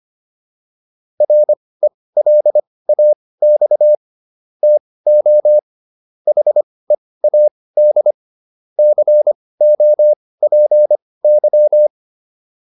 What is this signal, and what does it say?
Morse code, heard as a keyed tone, about 25 wpm, spaced as for 18 wpm, 600 Hz. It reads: RELAX TO HEAD COPY